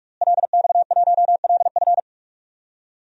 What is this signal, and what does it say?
Morse code, heard as a keyed tone, about 47 wpm, 700 Hz. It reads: LX1LF
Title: DXCC-100 Call Signs - 5 Characters - Repeated 3x - 45wpm